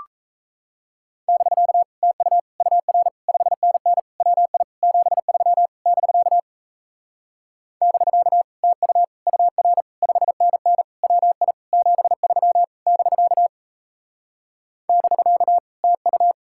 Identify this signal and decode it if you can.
Morse code, heard as a keyed tone, about 38 wpm, 700 Hz. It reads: <BK> TU UR 5NN WI 73 <BK> <BK> TU UR 5NN WI 73 <BK> <BK> TU